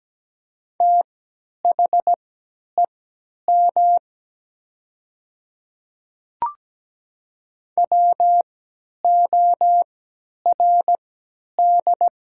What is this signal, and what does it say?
Morse code, heard as a keyed tone, about 17 wpm, spaced as for 10 wpm, 690 Hz. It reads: THEM WORD
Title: 3x Character Spacing - Top 100 Words - 17wpm